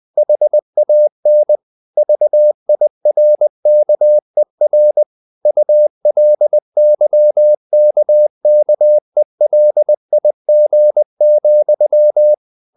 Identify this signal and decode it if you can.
Morse code, heard as a keyed tone, about 20 wpm, 600 Hz. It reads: HAN VIRKER ULYKKELIG,
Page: Ferie 20 wpm